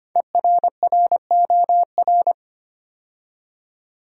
Morse code, heard as a keyed tone, about 25 wpm, 700 Hz. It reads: ERROR
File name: 5 Letter Words - First 3 Letters Given - 25wpm